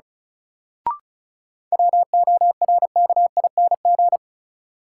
Morse code, heard as a keyed tone, about 36 wpm, 700 Hz. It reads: WORKING
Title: Sentences from Top 700 Words - New Word Drill - 35wpm